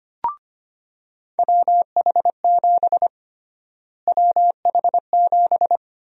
Morse code, heard as a keyed tone, about 25 wpm, 700 Hz. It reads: WH7 WH7
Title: USA Call Sign Prefixes - Repeated 2x - 25wpm